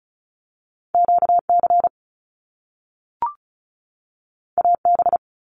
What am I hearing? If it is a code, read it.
Morse code, heard as a keyed tone, about 36 wpm, 700 Hz. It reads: QC AB
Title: Canadian Provinces and Territories 35wpm